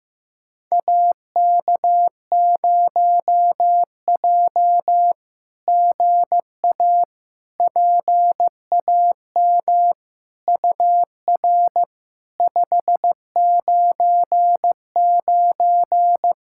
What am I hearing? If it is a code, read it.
Morse code, heard as a keyed tone, about 16 wpm, 700 Hz. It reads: AK0J GA PAM UR 599
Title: SOTA QSO Practice 15wpm